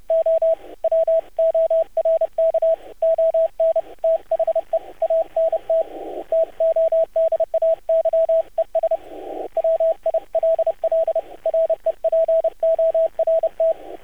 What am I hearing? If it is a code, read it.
Morse code, heard as a keyed tone, about 30 wpm, 650 Hz, with radio noise. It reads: O WORK ON THE ANT TODAYES WILL REPORT
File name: archived / CRSnet